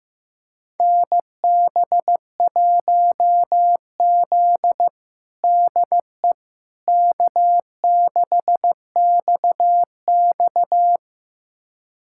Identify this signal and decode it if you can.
Morse code, heard as a keyed tone, about 14 wpm, 700 Hz. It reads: NB1Z DE K6XX
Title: USA Callsign DE Callsign - 4 Characters - 15wpm